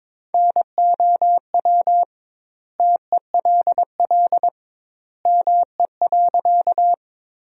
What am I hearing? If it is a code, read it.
Morse code, heard as a keyed tone, about 21 wpm, 700 Hz. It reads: NOW TELL ME.
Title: 2x Word-Spacing - Sentences from Top 200 Words - 22wpm